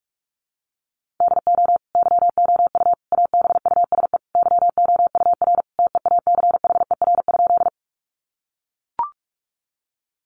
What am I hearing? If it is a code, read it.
Morse code, heard as a keyed tone, about 44 wpm, 700 Hz. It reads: DO YOU ABUSE YOUR TEACHER?
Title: Sentences from Top 1300 Words 45wpm